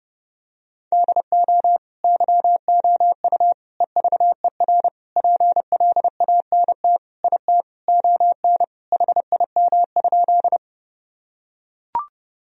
Morse code, heard as a keyed tone, about 32 wpm, 700 Hz. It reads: DO YOU EVER PLANT IT ON HIM?
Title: Sentences from Top 1000 Words - Review Part 1 - 30wpm